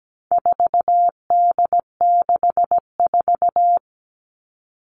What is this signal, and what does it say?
Morse code, heard as a keyed tone, about 17 wpm, 700 Hz. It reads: 4D64
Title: All Cut Numbers - 4-Digit 17wpm